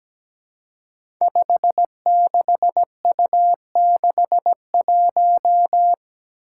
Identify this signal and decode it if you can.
Morse code, heard as a keyed tone, about 17 wpm, 700 Hz. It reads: 56U61
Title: Five Challenging Letter-Number Combinations 17wpm